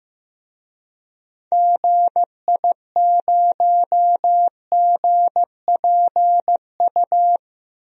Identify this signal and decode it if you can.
Morse code, heard as a keyed tone, about 15 wpm, 700 Hz. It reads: GI0GPU
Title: DXCC-100 Call Signs - 6 Characters - 15wpm